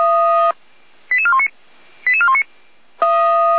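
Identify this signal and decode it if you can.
Five-tone selective call (ZVEI), heard as quick runs of five tones, tones of about 70 ms. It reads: 88419 88419
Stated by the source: Five-tone ZVEI selective call received on an SDR, from the Feuerwehr Orlamünde audio archive.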